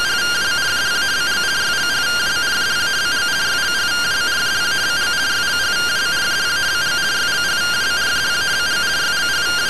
Two-tone teleprinter signal, about 170 Hz shift.